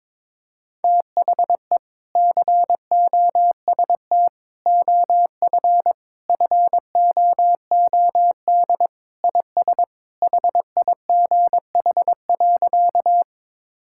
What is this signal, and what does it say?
Morse code, heard as a keyed tone, about 22 wpm, 700 Hz. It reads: THE COST OF FOOD IS HIGH.
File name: practice / 22wpm / Sentences with 3-Word Phrases - 22wpm